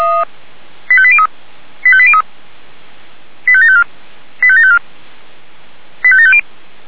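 Five-tone selective call (ZVEI), heard as quick runs of five tones, tones of about 70 ms. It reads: 85803 85803 85754 85754 85750
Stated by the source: ZVEI five-tone selective call 85827 received via SDR.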